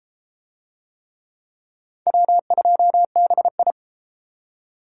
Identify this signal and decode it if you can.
Morse code, heard as a keyed tone, about 30 wpm, 700 Hz. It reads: W2BI